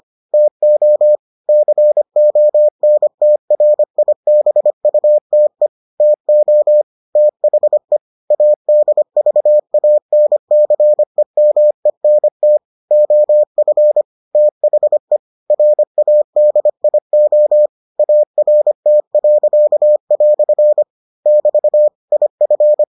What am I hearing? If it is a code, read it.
Morse code, heard as a keyed tone, about 25 wpm, 600 Hz. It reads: TO CONTRIBUTE TO THE ADVANCEMENT OF THE RADIO ART." <BT> IF